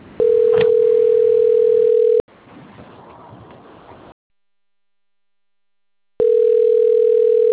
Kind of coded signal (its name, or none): telephone call progress tone